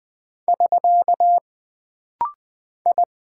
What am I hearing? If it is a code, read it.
Morse code, heard as a keyed tone, about 20 wpm, 700 Hz. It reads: <SK> I
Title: Single Character (Letter:Number:Punctuation:Prosign) - Rapid-Fire - 20wpm